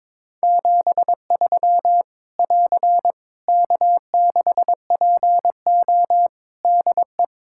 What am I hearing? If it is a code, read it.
Morse code, heard as a keyed tone, about 21 wpm, 700 Hz. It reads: 73 <AR> K6PO DE
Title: K1USN SST QSO Practice 22wpm